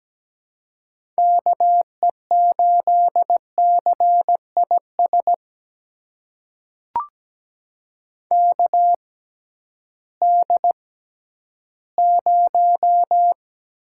Morse code, heard as a keyed tone, about 17 wpm, spaced as for 11 wpm, 700 Hz. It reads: KE8CIS KD0
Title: USA Call Signs - 6 Characters - 6x Character Spacing - 17wpm